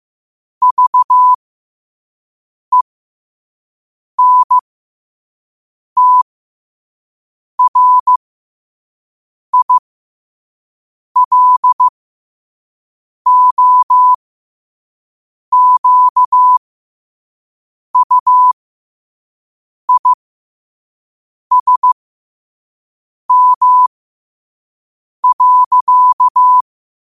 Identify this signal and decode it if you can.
Morse code, heard as a keyed tone, about 15 wpm, spaced as for 5 wpm, 1000 Hz. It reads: VENTRILOQUISM.
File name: 5wpm